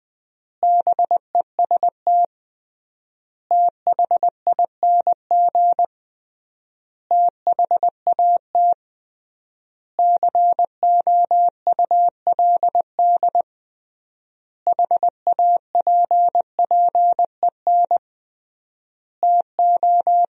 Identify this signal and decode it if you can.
Morse code, heard as a keyed tone, about 21 wpm, 700 Hz. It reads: BEST THING THAT COULD HAPPEN TO
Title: 3x Word-Spacing - Sentences from Top 400 Words - 20wpm